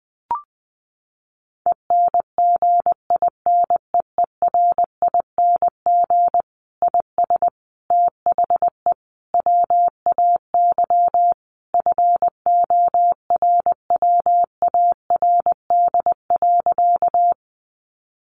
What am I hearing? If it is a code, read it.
Morse code, heard as a keyed tone, about 20 wpm, 700 Hz. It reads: ENGINEERING IS THE WAY FORWARD.
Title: Sentences from Top 1500 Words 20wpm